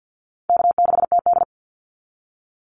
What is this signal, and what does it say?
Morse code, heard as a keyed tone, about 48 wpm, 700 Hz. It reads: K6TD
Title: Call Signs Repeated 3x - 50wpm